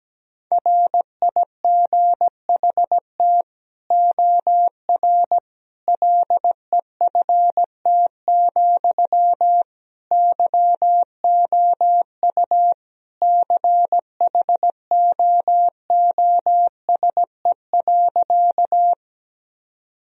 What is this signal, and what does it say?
Morse code, heard as a keyed tone, about 17 wpm, 700 Hz. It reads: RIGHT OR LEFT, YOU CHOOSE.
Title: Sentences from Top 1000 Words - Review Part 4 - 17wpm